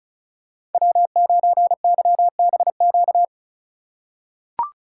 Morse code, heard as a keyed tone, about 34 wpm, 700 Hz. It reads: W9YBQ
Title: USA Call Signs - 2x Character Spacing - 35wpm